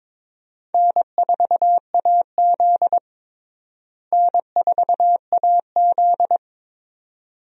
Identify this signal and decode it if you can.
Morse code, heard as a keyed tone, about 22 wpm, 700 Hz. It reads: N4AZ N4AZ
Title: USA Call Signs - 4 Characters - Repeated 2x - 22wpm